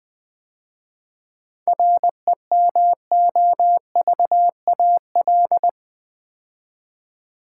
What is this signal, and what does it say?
Morse code, heard as a keyed tone, about 21 wpm, 700 Hz. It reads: REMOVAL
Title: 7 Letter Words - First 3 Letters Given - 20wpm